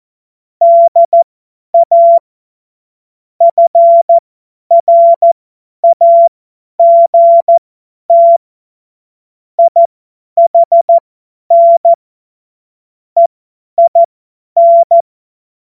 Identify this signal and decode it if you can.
Morse code, heard as a keyed tone, about 14 wpm, spaced as for 10 wpm, 680 Hz. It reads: DA FRAGT IHN EIN
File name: Witz-19-August-2025-50-F70